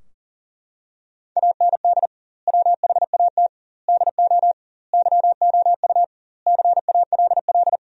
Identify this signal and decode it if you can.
Morse code, heard as a keyed tone, about 40 wpm, 700 Hz. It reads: AND WHAT DO YOU CALL
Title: 2x Word-Spacing - Sentences from Top 100 Words - 40wpm